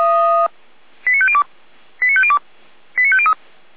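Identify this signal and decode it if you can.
Five-tone selective call (ZVEI), heard as quick runs of five tones, tones of about 70 ms. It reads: 89502 89502 89503
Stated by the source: ZVEI five-tone selective call received on SDR.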